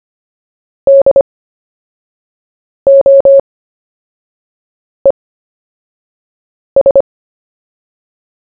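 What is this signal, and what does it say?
Morse code, heard as a keyed tone, about 25 wpm, spaced as for 5 wpm, 550 Hz. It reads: DOES
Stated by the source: CW story-telling practice, Short Story 1 sent at 5 wpm, by VK4JE, Queensland, Australia.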